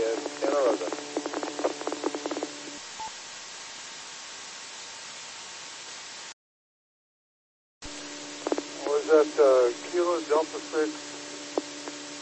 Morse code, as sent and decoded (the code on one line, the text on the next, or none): none